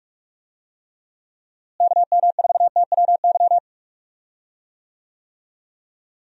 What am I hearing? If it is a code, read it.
Morse code, heard as a keyed tone, about 46 wpm, 700 Hz. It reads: KM4TWY
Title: USA Call Signs - 6 Characters - Repeated 2x - 45wpm